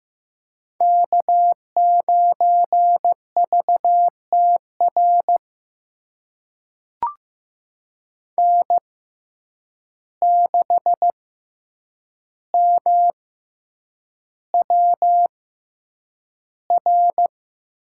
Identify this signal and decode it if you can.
Morse code, heard as a keyed tone, about 15 wpm, spaced as for 8 wpm, 700 Hz. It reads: K9VTR N6MWR
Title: USA Call Signs - 5 Characters - 6x Character Spacing - 15wpm